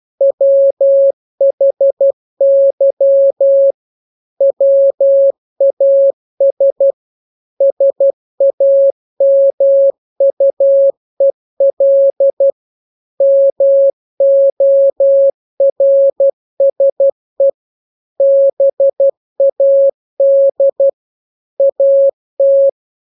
Morse code, heard as a keyed tone, about 12 wpm, 550 Hz. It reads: WHY WAS SAMUEL MORSE BAD AT
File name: jokes / joke 1 12 0000